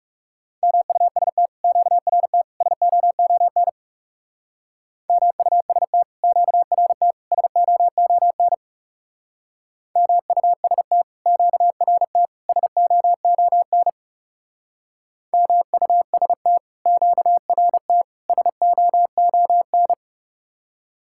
Morse code, heard as a keyed tone, about 35 wpm, 700 Hz. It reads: MUST QRT SOON MUST QRT SOON MUST QRT SOON MUST QRT SOON
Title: Speed-Racing Common QSO Phrases - 30wpm